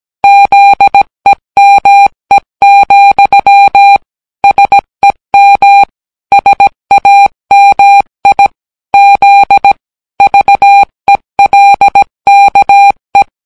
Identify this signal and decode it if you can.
Morse code, heard as a keyed tone, about 17 wpm, 800 Hz. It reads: ZEME, SEM SAMI Z VELKE